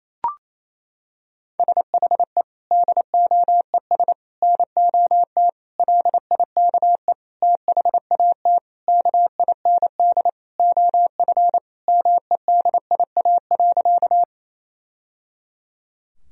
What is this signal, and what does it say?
Morse code, heard as a keyed tone, about 28 wpm, 700 Hz. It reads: SHE DOES NOT LIKE THAT KIND OF MEDIA.
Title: Sentences from Top 1000 Words - Review Part 4 - 28wpm